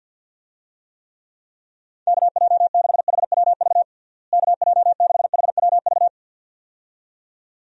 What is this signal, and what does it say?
Morse code, heard as a keyed tone, about 47 wpm, 700 Hz. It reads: KJ6HWV KJ6HWV
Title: USA Call Signs - 6 Characters - Repeated 2x - 50wpm